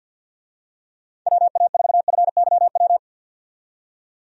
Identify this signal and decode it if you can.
Morse code, heard as a keyed tone, about 48 wpm, 700 Hz. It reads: WA4FYW
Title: USA Call Signs - 6 Characters - 50wpm